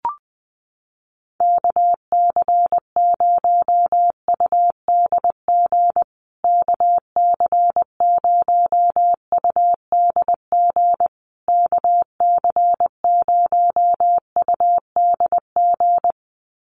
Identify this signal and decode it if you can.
Morse code, heard as a keyed tone, about 20 wpm, 700 Hz. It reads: KC0UDG KC0UDG KC0UDG
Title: USA Call Signs Repeated 3x 20wpm